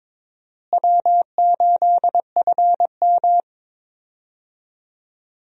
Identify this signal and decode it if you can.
Morse code, heard as a keyed tone, about 22 wpm, 700 Hz. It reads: W8FM
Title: USA Call Signs - 4 Characters - Repeated 2x - 22wpm